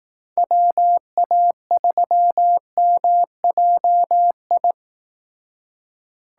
Morse code, heard as a keyed tone, about 18 wpm, 700 Hz. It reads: WA3MJI
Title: Speed Racing - USA 2x3 Call Signs 15wpm